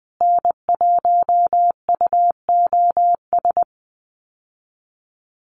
Morse code, heard as a keyed tone, about 20 wpm, 700 Hz. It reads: N1UOS
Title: USA Call Signs Repeated 3x 20wpm